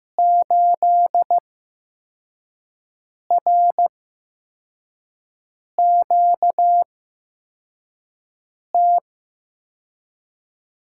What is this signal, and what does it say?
Morse code, heard as a keyed tone, about 15 wpm, spaced as for 4 wpm, 700 Hz. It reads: 8RQT
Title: USA Call Signs - 5 Characters - 8x Character Spacing - 15wpm